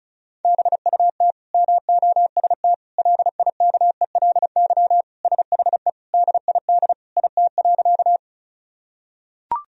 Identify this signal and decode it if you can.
Morse code, heard as a keyed tone, about 36 wpm, 700 Hz. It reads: BUT MOST LIKELY SHE DID IT.
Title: Sentences from Top 1000 Words - Review Part 4 - 35wpm